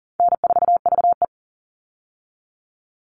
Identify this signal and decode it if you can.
Morse code, heard as a keyed tone, about 40 wpm, 700 Hz. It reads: N4VE
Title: All Cut Numbers - 4-Digit 40wpm